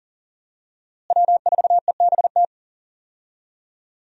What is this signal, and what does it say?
Morse code, heard as a keyed tone, about 40 wpm, 700 Hz. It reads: W4EBT
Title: USA Call Signs - 5 Characters - Repeated 3x - 40wpm